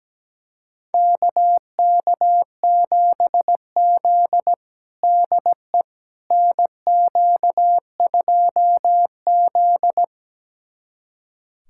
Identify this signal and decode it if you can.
Morse code, heard as a keyed tone, about 17 wpm, 700 Hz. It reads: KK7Z DE NQ2Z
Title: USA Callsign DE Callsign - 4 Characters - 17wpm